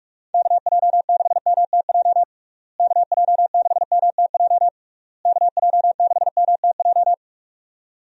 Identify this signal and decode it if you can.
Morse code, heard as a keyed tone, about 46 wpm, 700 Hz. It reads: KJ6MTJ KJ6MTJ KJ6MTJ
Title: USA Call Signs - 6 Characters - Repeated 3x - 45wpm